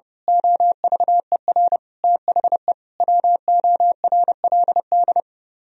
Morse code, heard as a keyed tone, about 28 wpm, 700 Hz. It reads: OVER THE WORLD
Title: Speed-Racing - Sets of 5 Words 30wpm